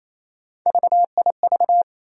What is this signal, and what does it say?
Morse code, heard as a keyed tone, about 28 wpm, 700 Hz. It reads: VIV
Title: USA Call Sign Postfixes - 3 Letters 28wpm